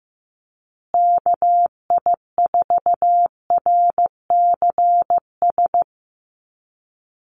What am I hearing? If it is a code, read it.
Morse code, heard as a keyed tone, about 15 wpm, 700 Hz. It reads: KI4RCS
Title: USA Call Signs Repeated 2x 15wpm